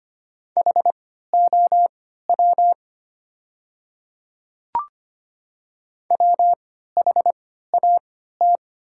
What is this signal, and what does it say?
Morse code, heard as a keyed tone, about 25 wpm, spaced as for 14 wpm, 690 Hz. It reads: HOW WHAT
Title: 3x Character Spacing - Top 300 Words - 25wpm